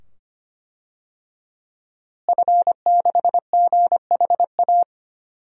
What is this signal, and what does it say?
Morse code, heard as a keyed tone, about 25 wpm, 700 Hz. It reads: F6GHA
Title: DXCC-100 Call Signs - 5 Characters - Repeated 2x - 25wpm